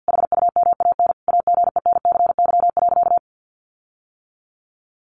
Morse code, heard as a keyed tone, about 50 wpm, 700 Hz. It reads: HUMAN AGENCY.
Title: Sentences from Top 1300 Words 50wpm